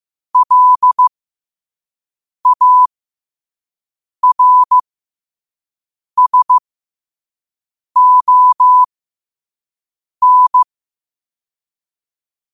Morse code, heard as a keyed tone, about 15 wpm, spaced as for 5 wpm, 1000 Hz. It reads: LARSON
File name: qotdfiles / 5wpm